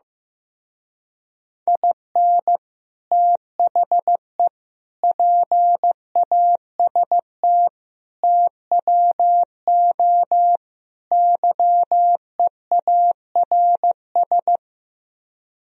Morse code, heard as a keyed tone, about 16 wpm, 700 Hz. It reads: IN THE PAST TWO YEARS
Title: Speed-Racing - Sets of 5 Words 15wpm